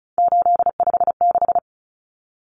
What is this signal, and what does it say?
Morse code, heard as a keyed tone, about 34 wpm, 700 Hz. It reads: 856
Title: Numbers - 3-Digit 35wpm